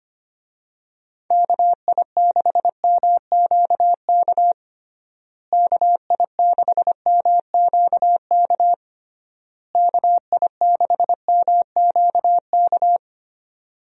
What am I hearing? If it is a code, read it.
Morse code, heard as a keyed tone, about 25 wpm, 700 Hz. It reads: KI6MQK KI6MQK KI6MQK
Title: USA Call Signs - 6 Characters - Repeated 3x - 25wpm